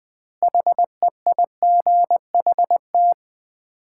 Morse code, heard as a keyed tone, about 20 wpm, 700 Hz. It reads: HEIGHT
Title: 4x Word-Spacing - Sets of 5 Words - 20wpm